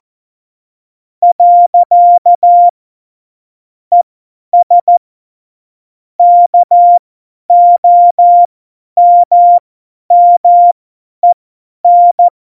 Morse code, heard as a keyed tone, about 14 wpm, spaced as for 10 wpm, 700 Hz. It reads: . ES KOMMEN
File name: Morsekurs / Witze / Witz 23. Oktober 2018, 50 F70